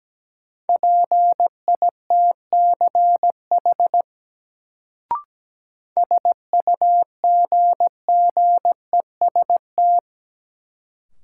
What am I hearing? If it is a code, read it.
Morse code, heard as a keyed tone, about 17 wpm, 700 Hz. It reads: PITCH SUGGEST
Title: Sentences from Top 1300 Words - New Word Drill Alt - 17wpm